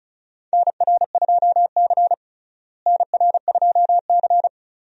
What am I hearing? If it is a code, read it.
Morse code, heard as a keyed tone, about 36 wpm, 700 Hz. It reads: NR2C NR2C
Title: USA Call Signs - 4 Characters - Repeated 3x - 35wpm